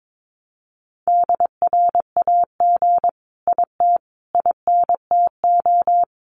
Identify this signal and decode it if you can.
Morse code, heard as a keyed tone, about 22 wpm, 700 Hz. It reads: DRAG IT INTO
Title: Sentences from Top 1500 Words 22wpm